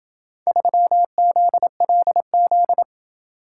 Morse code, heard as a keyed tone, about 27 wpm, 700 Hz. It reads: 3ZLZ